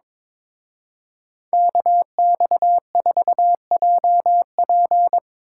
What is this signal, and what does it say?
Morse code, heard as a keyed tone, about 25 wpm, 700 Hz. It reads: KX4JP
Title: Speed Racing - USA Call Signs 22wpm